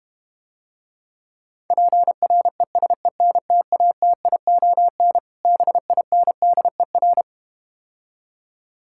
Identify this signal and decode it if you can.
Morse code, heard as a keyed tone, about 32 wpm, 700 Hz. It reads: PRESENTATION BINDER